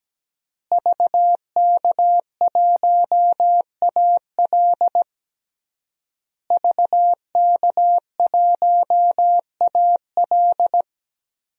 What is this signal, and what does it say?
Morse code, heard as a keyed tone, about 17 wpm, 700 Hz. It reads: VK1AL VK1AL
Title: DXCC-100 Call Signs - 5 Characters - Repeated 3x - 17wpm